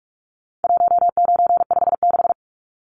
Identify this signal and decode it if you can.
Morse code, heard as a keyed tone, about 44 wpm, 700 Hz. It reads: 1956
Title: Numbers - 4-Digit 45wpm